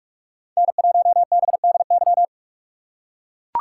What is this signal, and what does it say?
Morse code, heard as a keyed tone, about 43 wpm, 700 Hz. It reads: N1BDY